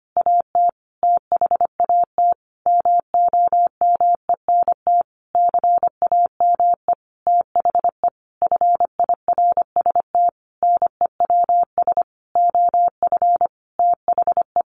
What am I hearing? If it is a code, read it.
Morse code, heard as a keyed tone, about 25 wpm, 700 Hz. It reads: AT THAT MOMENT CAME THE FIRST NEWS OF THE
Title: Sentences from Top 1500 Words 25wpm